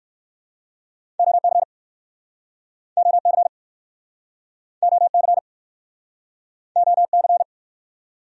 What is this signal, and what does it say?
Morse code, heard as a keyed tone, about 60 wpm, 700 Hz. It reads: OC OC OC OC